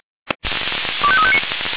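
Five-tone selective call (ZVEI), heard as quick runs of five tones, tones of about 70 ms. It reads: 25359
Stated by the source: ZVEI five-tone selective call received on SDR.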